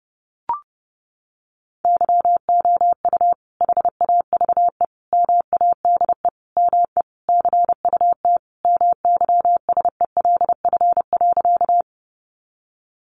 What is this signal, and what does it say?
Morse code, heard as a keyed tone, about 30 wpm, 700 Hz. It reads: YOU HAVE MADE ME CUT MYSELF.
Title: Sentences from Top 1300 Words 30wpm